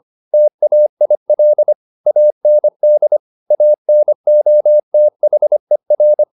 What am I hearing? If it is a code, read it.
Morse code, heard as a keyed tone, about 25 wpm, 600 Hz. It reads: TAIL AND ANOTHER